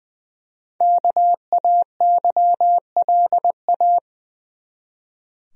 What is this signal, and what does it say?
Morse code, heard as a keyed tone, about 20 wpm, 700 Hz. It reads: KAYLA
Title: USA Common Names - 20wpm